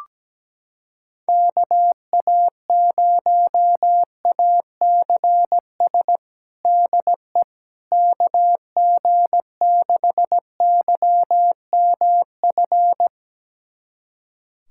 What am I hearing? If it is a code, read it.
Morse code, heard as a keyed tone, about 17 wpm, 700 Hz. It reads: KA0ACS DE KG6YMF
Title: USA Callsign DE Callsign 17wpm